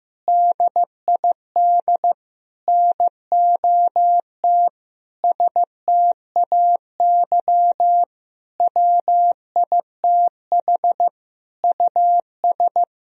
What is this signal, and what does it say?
Morse code, heard as a keyed tone, about 15 wpm, 700 Hz. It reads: DID NOT STAY WITH US
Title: Sentences from Top 1000 Words - Review Part 2 - 15wpm